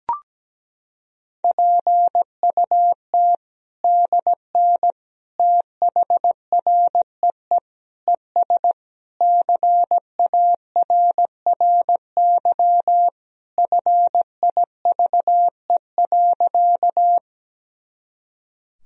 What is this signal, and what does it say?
Morse code, heard as a keyed tone, about 17 wpm, 700 Hz. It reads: PUT DN THREE ES CARRY FIVE.
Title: Sentences from Top 400 Words and Abbreviations 17wpm